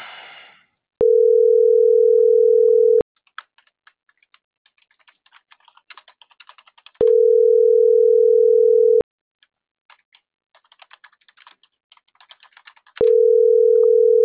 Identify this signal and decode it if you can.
Ringback tone (North America), 440 + 480 Hz.